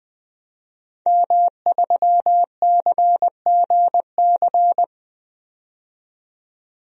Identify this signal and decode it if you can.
Morse code, heard as a keyed tone, about 20 wpm, 700 Hz. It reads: M3CGC
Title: DXCC-100 Call Signs - 5 Characters - 20wpm